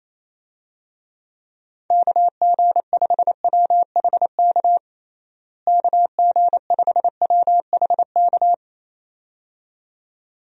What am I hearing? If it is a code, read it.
Morse code, heard as a keyed tone, about 27 wpm, 700 Hz. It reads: KG5WHK KG5WHK
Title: USA Call Signs - 6 Characters - Repeated 2x - 28wpm